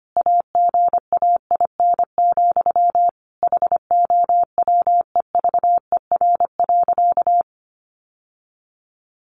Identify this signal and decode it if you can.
Morse code, heard as a keyed tone, about 25 wpm, 700 Hz. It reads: AGAIN, HOWEVER.
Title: Sentences from Top 600 Words 25wpm